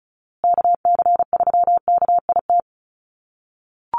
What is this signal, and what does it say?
Morse code, heard as a keyed tone, about 34 wpm, 700 Hz. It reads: KC3KIT
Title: USA Call Signs Repeated 3x 35wpm